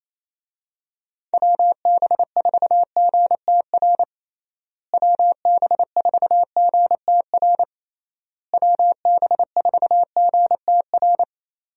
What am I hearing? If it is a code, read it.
Morse code, heard as a keyed tone, about 28 wpm, 700 Hz. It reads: WB4GTR WB4GTR WB4GTR
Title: USA Call Signs - 6 Characters - Repeated 3x - 28wpm